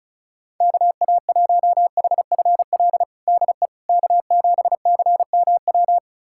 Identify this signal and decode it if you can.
Morse code, heard as a keyed tone, about 34 wpm, 700 Hz. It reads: KA1HFL DE K7CMW
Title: USA Callsign DE Callsign 35wpm